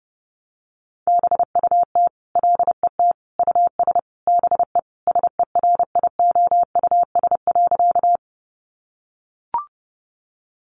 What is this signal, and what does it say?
Morse code, heard as a keyed tone, about 30 wpm, 700 Hz. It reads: BUT LET US BE SERIOUS.
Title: Sentences from Top 1100 Words 30wpm